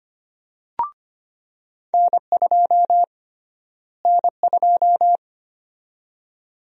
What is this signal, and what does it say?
Morse code, heard as a keyed tone, about 25 wpm, 700 Hz. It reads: N2 N2
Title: USA Call Sign Prefixes - Repeated 2x - 25wpm